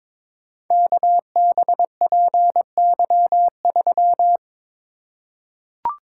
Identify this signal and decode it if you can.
Morse code, heard as a keyed tone, about 22 wpm, 700 Hz. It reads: KBPY3
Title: Code Groups - 5 Letter:Numbers - 22wpm